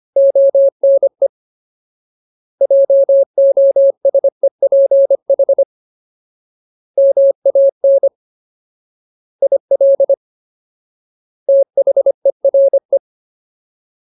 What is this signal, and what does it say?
Morse code, heard as a keyed tone, about 25 wpm, 550 Hz. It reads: ONE JOSEPH MAN IL THERE
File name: Ultimate Word List / 25WPM Ultimate